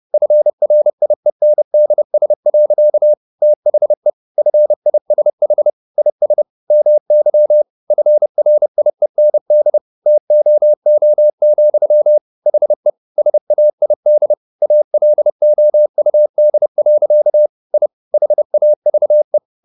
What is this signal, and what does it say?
Morse code, heard as a keyed tone, about 30 wpm, 600 Hz. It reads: FRIENDS. THE FISH IS MY FRIEND TOO, HE SAID ALOUD. I HAVE